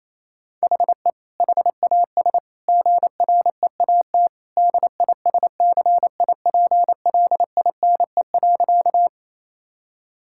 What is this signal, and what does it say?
Morse code, heard as a keyed tone, about 27 wpm, 700 Hz. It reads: HE HAS GREAT DISCIPLINE.
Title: Sentences from Top 1000 Words - Review Part 6 - 28wpm